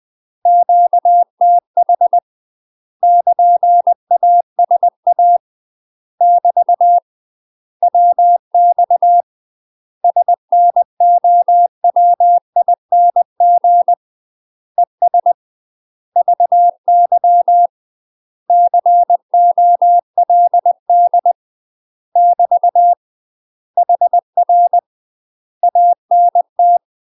Morse code, heard as a keyed tone, about 20 wpm, 700 Hz. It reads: QTH <KN> ASA <BT> WX SNOWING ES VY COLD <BT> HR ANT
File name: UK MASTER QSOs TRAINING / 20WPM / UK MASTER QSOs 0008